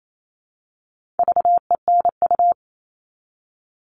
Morse code, heard as a keyed tone, about 28 wpm, 700 Hz. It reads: VENU
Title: All Cut Numbers - 4-Digit 28wpm